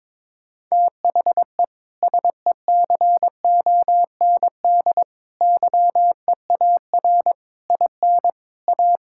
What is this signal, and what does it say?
Morse code, heard as a keyed tone, about 22 wpm, 700 Hz. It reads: THE SECOND YEAR IN A
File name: practice / 22wpm / Sets of 6 Words - 22wpm